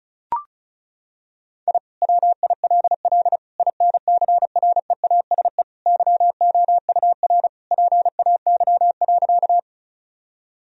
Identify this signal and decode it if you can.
Morse code, heard as a keyed tone, about 36 wpm, 700 Hz. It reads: I WILL INCREASE YOUR PAY.
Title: Sentences from Top 1000 Words - Review Part 4 - 35wpm